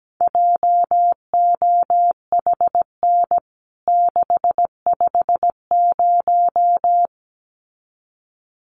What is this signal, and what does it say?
Morse code, heard as a keyed tone, about 17 wpm, 700 Hz. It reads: JOHN 650
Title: CWT Exchange 17wpm